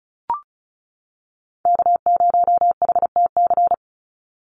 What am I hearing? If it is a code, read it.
Morse code, heard as a keyed tone, about 34 wpm, 700 Hz. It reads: K0HTC
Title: USA Call Signs 35wpm